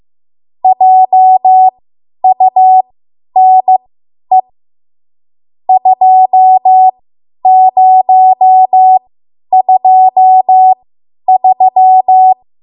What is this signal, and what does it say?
Morse code, heard as a keyed tone, about 15 wpm, spaced as for 10 wpm, 750 Hz. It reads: JUNE 2023
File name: Morse / 10WPM